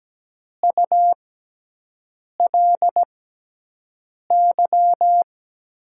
Morse code, heard as a keyed tone, about 17 wpm, spaced as for 6 wpm, 700 Hz. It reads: ULY
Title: USA Call Signs - 6x Character Spacing - 17wpm